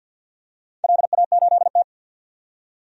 Morse code, heard as a keyed tone, about 50 wpm, 700 Hz. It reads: LA8T
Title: DXCC-100 Call Signs - 4 Characters - 50wpm